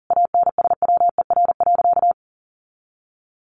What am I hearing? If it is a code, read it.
Morse code, heard as a keyed tone, about 40 wpm, 700 Hz. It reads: ANSWER.
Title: Sentences from Top 1500 Words 40wpm